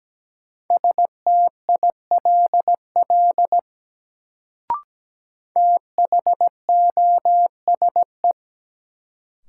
Morse code, heard as a keyed tone, about 17 wpm, 700 Hz. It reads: STILL THOSE
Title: Sentences from Top 300 Words - New Word Drill - 17wpm